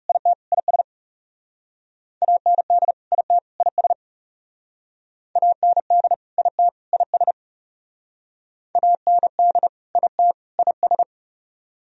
Morse code, heard as a keyed tone, about 40 wpm, 700 Hz. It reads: IT IS AND IT IS AND IT IS AND IT IS